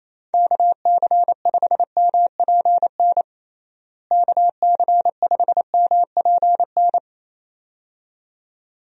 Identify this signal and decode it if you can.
Morse code, heard as a keyed tone, about 28 wpm, 700 Hz. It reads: KC5MPN KC5MPN
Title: USA Call Signs - 6 Characters - Repeated 2x - 28wpm